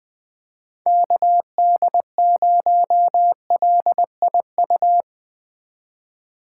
Morse code, heard as a keyed tone, about 20 wpm, 700 Hz. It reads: KD0LIU